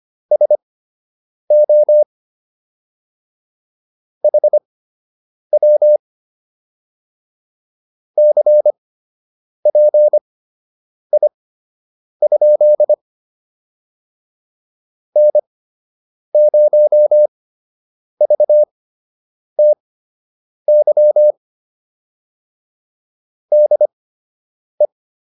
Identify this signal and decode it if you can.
Morse code, heard as a keyed tone, about 25 wpm, spaced as for 8 wpm, 600 Hz. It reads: SO HW CPI? N0VTY DE